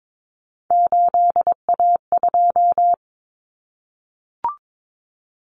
Morse code, heard as a keyed tone, about 22 wpm, 700 Hz. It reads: 8A2
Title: Common Cut Numbers - 3-Digit 22wpm